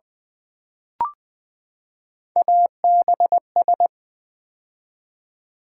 Morse code, heard as a keyed tone, about 21 wpm, 690 Hz. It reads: ABS
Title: Build A Word - 3 Letters At A Time - 8-Letter Words - 20wpm